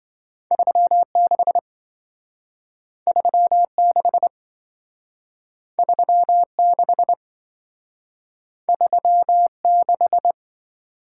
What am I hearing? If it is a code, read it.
Morse code, heard as a keyed tone, about 27 wpm, 700 Hz. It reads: 36 36 36 36